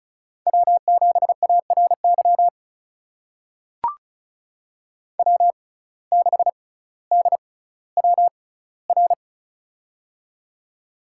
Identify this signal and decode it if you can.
Morse code, heard as a keyed tone, about 36 wpm, spaced as for 18 wpm, 700 Hz. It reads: W7ARY W6DWR
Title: USA Call Signs - 5 Characters - 6x Character Spacing - 35wpm